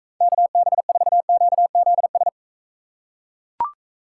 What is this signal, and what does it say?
Morse code, heard as a keyed tone, about 41 wpm, 700 Hz. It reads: KB4QZS